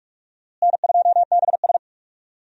Morse code, heard as a keyed tone, about 41 wpm, 700 Hz. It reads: N2BS
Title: USA Call Signs - 4 Characters - Repeated 3x - 45wpm